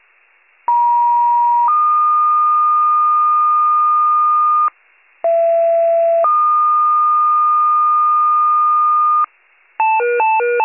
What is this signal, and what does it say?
Two-tone page, heard as a one-second tone, then a three-second tone, about 950 then 1220 Hz, about 670 then 1220 Hz. It is heard from 0.7 to 9.2 s.